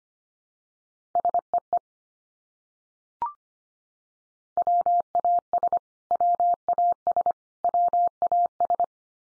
Morse code, heard as a keyed tone, about 25 wpm, 700 Hz. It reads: SEE WAS WAS WAS
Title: Top 100 Words 25wpm x3